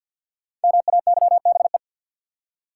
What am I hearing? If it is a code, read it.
Morse code, heard as a keyed tone, about 46 wpm, 700 Hz. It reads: MAYBE
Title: Five Letter Words - 50wpm